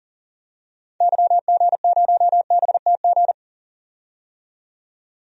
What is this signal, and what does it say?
Morse code, heard as a keyed tone, about 37 wpm, 700 Hz. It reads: YG0BTG
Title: DXCC-100 Call Signs - 6 Characters - 40wpm